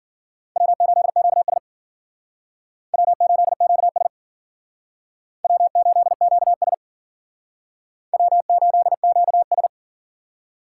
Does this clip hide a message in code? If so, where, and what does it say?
Yes: Morse code from 0.6 to 9.7 s, heard as a keyed tone, about 51 wpm, 700 Hz. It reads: W8QS W8QS W8QS W8QS